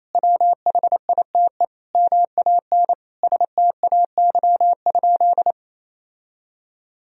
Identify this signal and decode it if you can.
Morse code, heard as a keyed tone, about 28 wpm, 700 Hz. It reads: WHITE MAN STAY?
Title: Sentences from Top 1000 Words - Review Part 1 - 28wpm